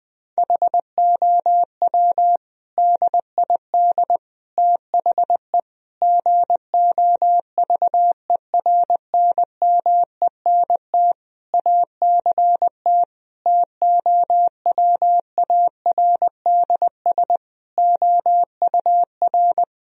Morse code, heard as a keyed tone, about 20 wpm, 700 Hz. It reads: HOW DID THE GOVERNMENT ACT TOWARDS OUR
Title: Sentences from Top 1000 Words - Review Part 4 - 20wpm